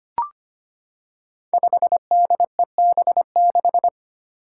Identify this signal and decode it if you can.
Morse code, heard as a keyed tone, about 25 wpm, 700 Hz. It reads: 5DEB6
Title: Five Challenging Letter-Number Combinations 25wpm